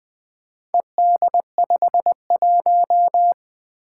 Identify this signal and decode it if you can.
Morse code, heard as a keyed tone, about 20 wpm, 700 Hz. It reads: ED51
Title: Four Challenging Letter-Number Combinations 20wpm